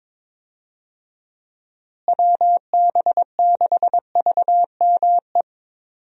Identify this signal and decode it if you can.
Morse code, heard as a keyed tone, about 22 wpm, 700 Hz. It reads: WB6VME
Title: USA Call Signs - 6 Characters - Repeated 2x - 22wpm